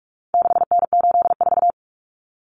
Morse code, heard as a keyed tone, about 44 wpm, 700 Hz. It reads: 6N84
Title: Common Cut Numbers - 4-Digit 45wpm